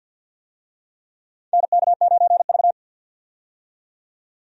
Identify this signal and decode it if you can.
Morse code, heard as a keyed tone, about 48 wpm, 700 Hz. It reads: NK9V
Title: USA Call Signs - 4 Characters - 50wpm